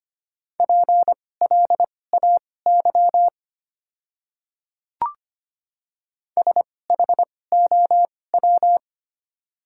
Morse code, heard as a keyed tone, about 25 wpm, spaced as for 18 wpm, 700 Hz. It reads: PLAY SHOW
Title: 2x Character Spacing - Top 200 Words - 25wpm